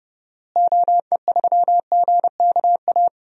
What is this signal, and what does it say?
Morse code, heard as a keyed tone, about 30 wpm, 700 Hz. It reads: OE3GKA
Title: DXCC-100 Call Signs - 6 Characters - Repeated 2x - 30wpm